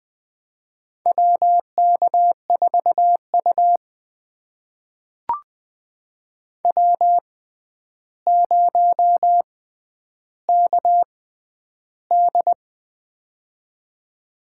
Morse code, heard as a keyed tone, about 20 wpm, spaced as for 10 wpm, 700 Hz. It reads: WK4U W0KD
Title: USA Call Signs - 4 Characters - 6x Character Spacing - 20wpm